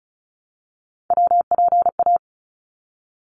Morse code, heard as a keyed tone, about 34 wpm, 700 Hz. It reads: WPA
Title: ARRL Sections 35wpm